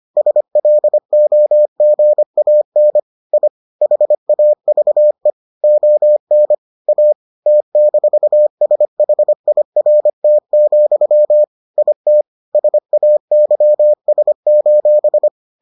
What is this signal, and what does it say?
Morse code, heard as a keyed tone, about 25 wpm, 600 Hz. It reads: SLOGAN I HAVE ON A T-SHIRT, IT SAYS: